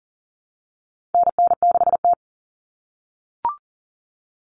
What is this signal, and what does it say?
Morse code, heard as a keyed tone, about 40 wpm, 700 Hz. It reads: NN6T
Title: Call Signs - 40wpm